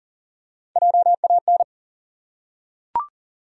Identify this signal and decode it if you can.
Morse code, heard as a keyed tone, about 40 wpm, 700 Hz. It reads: JAN
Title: USA Common Names - Max 3 Letters - 40wpm